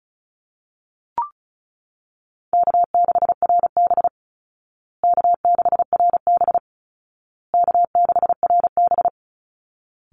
Morse code, heard as a keyed tone, about 34 wpm, 700 Hz. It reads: K6RB K6RB K6RB
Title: Call Signs Repeated 2x - 35wpm